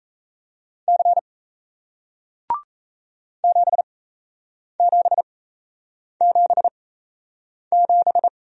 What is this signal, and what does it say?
Morse code, heard as a keyed tone, about 38 wpm, spaced as for 8 wpm, 700 Hz. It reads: C 7777